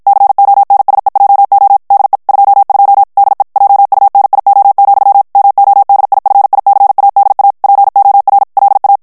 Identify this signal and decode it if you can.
Morse code, heard as a keyed tone, about 53 wpm, 800 Hz. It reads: CONSEJO DE 22 DE JUNIO, MODIFICADA POR LA
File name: morse mp3 / ESPAGNOL TEXTE EN CLAIR 2 VITESSE 50 MOTS MINUTES